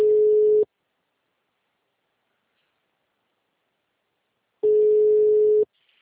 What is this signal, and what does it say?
Ringback tone (Europe, 425 Hz).
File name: out- -2035- - - .32073